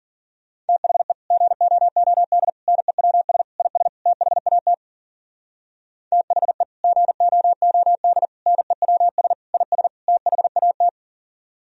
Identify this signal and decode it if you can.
Morse code, heard as a keyed tone, about 53 wpm, 700 Hz. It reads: THE GOOD NEWS IS THAT THE GOOD NEWS IS THAT